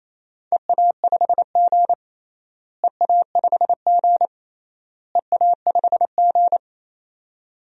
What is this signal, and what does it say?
Morse code, heard as a keyed tone, about 28 wpm, 700 Hz. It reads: EA5G EA5G EA5G
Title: DXCC-100 Call Signs - 4 Characters - Repeated 3x - 28wpm